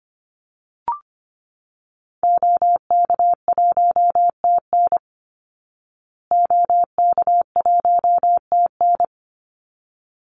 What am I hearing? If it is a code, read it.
Morse code, heard as a keyed tone, about 25 wpm, 700 Hz. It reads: OK1TN OK1TN
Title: Call Signs Repeated 3x - 25wpm